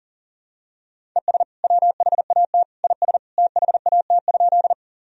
Morse code, heard as a keyed tone, about 40 wpm, 700 Hz. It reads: ES WHAT IS THAT?
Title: Sentences from Top 100 Words and Abbreviations 40wpm